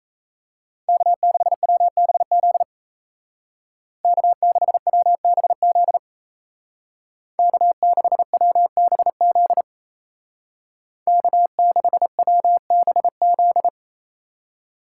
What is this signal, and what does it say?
Morse code, heard as a keyed tone, about 37 wpm, 700 Hz. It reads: K6WBZ K6WBZ K6WBZ K6WBZ